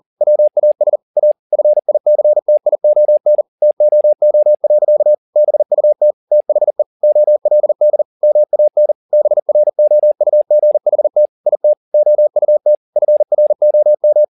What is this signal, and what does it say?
Morse code, heard as a keyed tone, about 40 wpm, 600 Hz. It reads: WAS A FICTION TOO. BUT THE OLD MAN BROUGHT IT OUT FROM